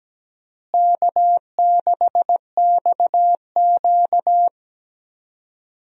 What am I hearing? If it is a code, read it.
Morse code, heard as a keyed tone, about 17 wpm, 700 Hz. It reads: K6XQ
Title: USA Call Signs - 4 Characters - Repeated 3x - 17wpm